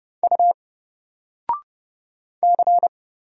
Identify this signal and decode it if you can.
Morse code, heard as a keyed tone, about 30 wpm, 700 Hz. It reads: U C
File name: Single-Letter-Number-30wpm